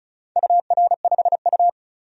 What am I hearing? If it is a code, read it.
Morse code, heard as a keyed tone, about 36 wpm, 700 Hz. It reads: UR5U
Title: DXCC-100 Call Signs - 4 Characters - 35wpm